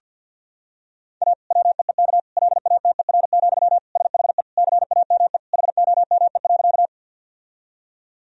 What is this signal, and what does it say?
Morse code, heard as a keyed tone, about 50 wpm, 700 Hz. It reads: A WEEK LATER, SHE CAME HOME.
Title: Sentences from Top 1000 Words - Review Part 4 - 50wpm